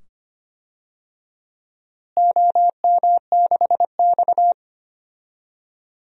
Morse code, heard as a keyed tone, about 25 wpm, 700 Hz. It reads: OM6X
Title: DXCC-100 Call Signs - 4 Characters - Repeated 2x - 25wpm